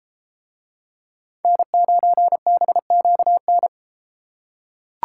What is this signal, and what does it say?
Morse code, heard as a keyed tone, about 31 wpm, 700 Hz. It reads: N9BQN